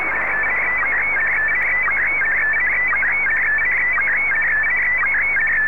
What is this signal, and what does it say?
SSTV (slow-scan television), Scottie DX.
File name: sstv / SDX-VK3AOB